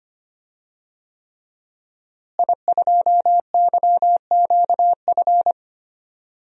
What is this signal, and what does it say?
Morse code, heard as a keyed tone, about 25 wpm, 700 Hz. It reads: I2YQF
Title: DXCC-100 Call Signs - 5 Characters - Repeated 3x - 25wpm